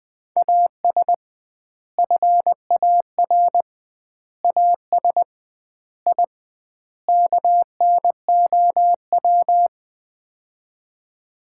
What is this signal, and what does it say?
Morse code, heard as a keyed tone, about 20 wpm, 700 Hz. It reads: AS FAR AS I KNOW
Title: 2x Word-Spacing - Sets of 5 Words - 20wpm